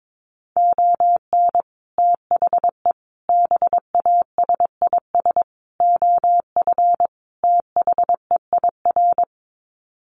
Morse code, heard as a keyed tone, about 22 wpm, 700 Hz. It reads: ON THE BASIS OF THEIR
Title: Sets of 5 Words Encore 22wpm